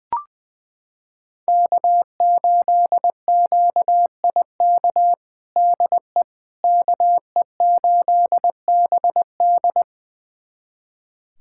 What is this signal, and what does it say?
Morse code, heard as a keyed tone, about 20 wpm, 700 Hz. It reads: K8QIK DE KE8BD
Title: USA Callsign DE Callsign - 5 Characters - 20wpm